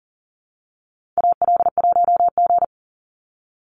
Morse code, heard as a keyed tone, about 40 wpm, 700 Hz. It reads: AL1G
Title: Call Signs Repeated 2x - 40wpm